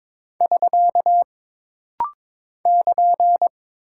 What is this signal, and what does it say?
Morse code, heard as a keyed tone, about 22 wpm, 700 Hz. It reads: <SK> <KN>
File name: practice / 22wpm / Single Punctuation-Prosign - Rapid-Fire - 22wpm